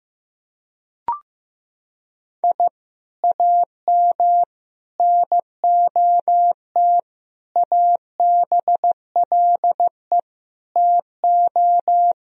Morse code, heard as a keyed tone, about 15 wpm, 700 Hz. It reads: I AM NOT ABLE TO
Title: Sentences from Top 1000 Words - Review Part 2 - 15wpm